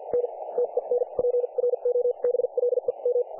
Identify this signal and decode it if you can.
Morse code, heard as a keyed tone, about 50 wpm, spaced as for 40 wpm, 500 Hz, with radio noise. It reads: N TEN PROBLEM
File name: dlazaawansowanych / nagraniazpasma / websdr recording 3564.7kHz